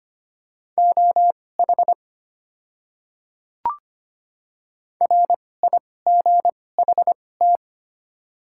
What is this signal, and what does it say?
Morse code, heard as a keyed tone, about 25 wpm, spaced as for 18 wpm, 690 Hz. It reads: OH RIGHT
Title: 2x Character Spacing - Top 500 Words - 25wpm